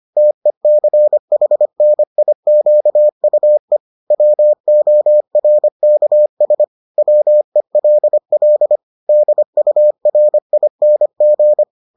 Morse code, heard as a keyed tone, about 25 wpm, 600 Hz. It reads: TECHNIQUE WORKS WELL DURING